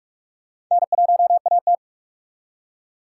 Morse code, heard as a keyed tone, about 45 wpm, 700 Hz. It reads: N1AT